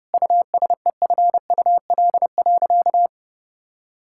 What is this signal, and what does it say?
Morse code, heard as a keyed tone, about 30 wpm, 700 Hz. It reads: USEFUL.
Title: Sentences from Top 1000 Words - Review Part 5 - 30wpm